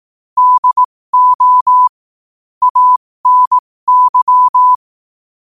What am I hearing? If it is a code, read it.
Morse code, heard as a keyed tone, about 18 wpm, spaced as for 15 wpm, 1000 Hz. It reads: DO ANY
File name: qotdfiles / 15wpm